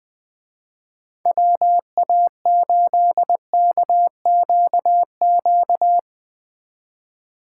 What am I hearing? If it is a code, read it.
Morse code, heard as a keyed tone, about 20 wpm, 700 Hz. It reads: WA8KQQ